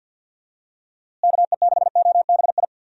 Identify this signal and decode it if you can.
Morse code, heard as a keyed tone, about 50 wpm, 700 Hz. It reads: KE6OBI
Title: USA Call Signs - 6 Characters - Repeated 3x - 50wpm